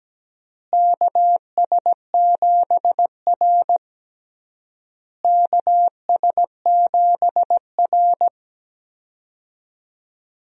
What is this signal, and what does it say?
Morse code, heard as a keyed tone, about 17 wpm, 700 Hz. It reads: KS7R KS7R
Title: USA Call Signs - 4 Characters - Repeated 3x - 17wpm